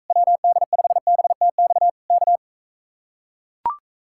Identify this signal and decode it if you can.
Morse code, heard as a keyed tone, about 44 wpm, 700 Hz. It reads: WD5BTX K